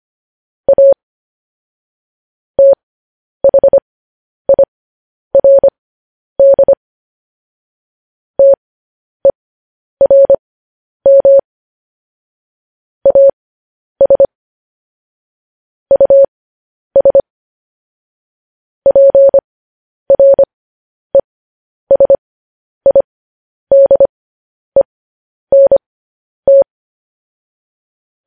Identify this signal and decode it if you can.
Morse code, heard as a keyed tone, about 25 wpm, spaced as for 10 wpm, 550 Hz. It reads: A THIRD TERM AS US PRESIDENT
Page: news-headlines-in-morse-code-at-10-wpm